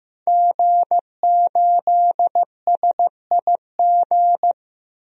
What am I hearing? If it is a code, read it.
Morse code, heard as a keyed tone, about 15 wpm, 700 Hz. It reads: G8SIG
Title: DXCC-100 Call Signs - 5 Characters - Repeated 3x - 15wpm